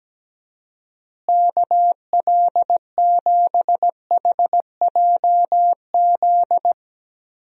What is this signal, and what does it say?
Morse code, heard as a keyed tone, about 17 wpm, 700 Hz. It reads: KL7HJZ
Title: DXCC-100 Call Signs - 6 Characters - Repeated 2x - 17wpm